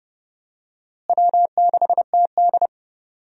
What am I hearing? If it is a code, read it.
Morse code, heard as a keyed tone, about 32 wpm, 700 Hz. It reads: W6TD
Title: USA Call Signs - 4 Characters - Repeated 2x - 30wpm